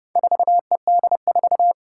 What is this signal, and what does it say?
Morse code, heard as a keyed tone, about 30 wpm, 700 Hz. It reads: 4ED4
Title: Four Challenging Letter-Number Combinations 30wpm